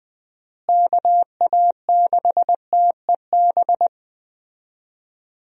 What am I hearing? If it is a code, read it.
Morse code, heard as a keyed tone, about 20 wpm, 700 Hz. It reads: KA6TEB
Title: USA Call Signs - 6 Characters - Repeated 3x - 20wpm